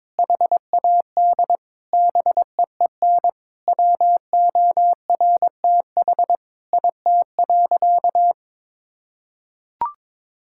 Morse code, heard as a keyed tone, about 22 wpm, 700 Hz. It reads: HAD BEEN WORTH IT.
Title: Sentences from Top 1000 Words - Review Part 5 - 22wpm